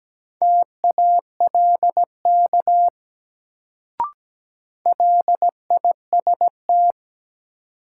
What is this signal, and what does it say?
Morse code, heard as a keyed tone, about 17 wpm, 700 Hz. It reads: TALK LIST
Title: Sentences from Top 400 Words - New Word Drill - 17wpm